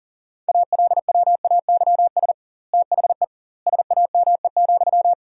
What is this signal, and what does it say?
Morse code, heard as a keyed tone, about 37 wpm, 700 Hz. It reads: ALWAYS THE SAME,
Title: 2x Word-Spacing - Sentences from Top 300 Words - 40wpm